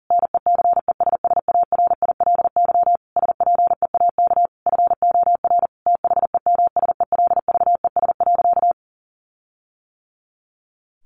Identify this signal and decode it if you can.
Morse code, heard as a keyed tone, about 40 wpm, 700 Hz. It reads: NECESSARILY SPEAK FOR THEMSELVES.
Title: Sentences from Top 1400 Words 40wpm